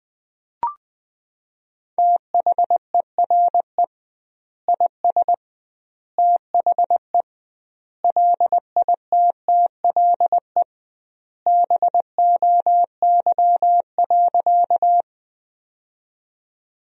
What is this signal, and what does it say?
Morse code, heard as a keyed tone, about 20 wpm, 700 Hz. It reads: THERE IS THE LITTLE BOY.
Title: 2x Word-Spacing - Sentences from Top 200 Words - 20wpm